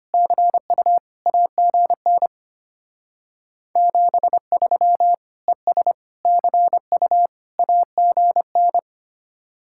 Morse code, heard as a keyed tone, about 31 wpm, 700 Hz. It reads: CU AGN 73 ES CU AGN
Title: Speed-Racing Common QSO Phrases - 25wpm